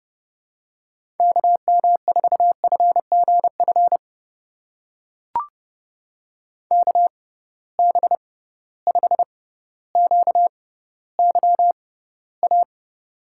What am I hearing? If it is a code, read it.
Morse code, heard as a keyed tone, about 32 wpm, spaced as for 16 wpm, 700 Hz. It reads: KM4FGF KB5QYA
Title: USA Call Signs - 6 Characters - 6x Character Spacing - 30wpm